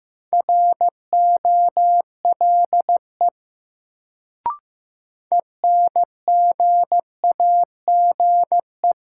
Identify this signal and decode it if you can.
Morse code, heard as a keyed tone, about 15 wpm, 700 Hz. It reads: ROLE ENGAGE
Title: Sentences from Top 1500 Words - New Word Drill Alt - 15wpm